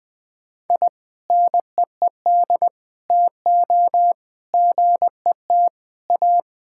Morse code, heard as a keyed tone, about 20 wpm, 700 Hz. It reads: I NEED TO GET A
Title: Sentences with 2-Word Phrases - Spoken Beginning - 20wpm